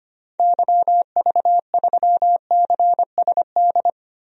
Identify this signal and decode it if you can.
Morse code, heard as a keyed tone, about 25 wpm, 700 Hz. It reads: YV3CSD
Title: DXCC-100 Call Signs - 6 Characters - Repeated 2x - 25wpm